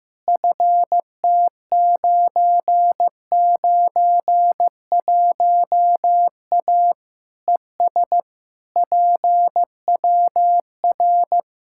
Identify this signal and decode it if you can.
Morse code, heard as a keyed tone, about 15 wpm, 700 Hz. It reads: FT991A ES PWR
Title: Practice QSO - Exchange 2 - Info - 15wpm